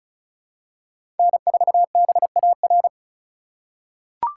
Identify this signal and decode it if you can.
Morse code, heard as a keyed tone, about 36 wpm, 700 Hz. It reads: N4BAR
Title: USA Call Signs - 5 Characters - 4x Character Spacing - 35wpm